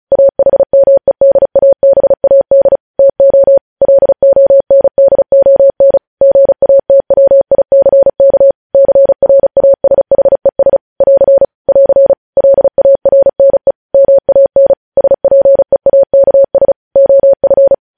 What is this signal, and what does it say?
Morse code, heard as a keyed tone, about 36 wpm, 550 Hz. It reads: AHMEDABAD TO LONDON GATWICK CRASHES <AR> <AR> LARNE MAN SPEAKS OF
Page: news headlines in morse code at 35 wpm